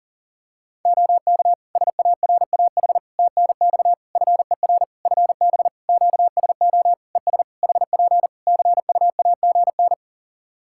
Morse code, heard as a keyed tone, about 40 wpm, 700 Hz. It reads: OK SARAH TNX FER FB QSO ES HP CUAGN
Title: Practice QSO - Exchange 4 - Ending - 40wpm